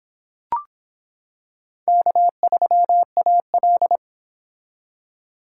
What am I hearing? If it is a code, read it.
Morse code, heard as a keyed tone, about 25 wpm, 700 Hz. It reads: K3AL